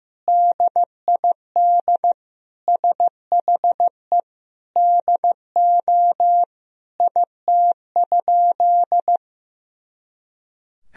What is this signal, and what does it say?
Morse code, heard as a keyed tone, about 15 wpm, 700 Hz. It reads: DID SHE DO IT?
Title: Sentences from Top 100 Words and Abbreviations 15wpm